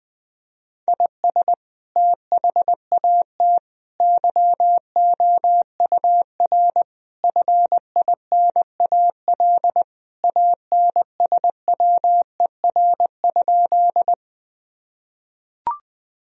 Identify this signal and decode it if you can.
Morse code, heard as a keyed tone, about 20 wpm, 700 Hz. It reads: IS THAT YOUR FINAL ANSWER?
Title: Sentences from Top 1000 Words - Review Part 2 - 20wpm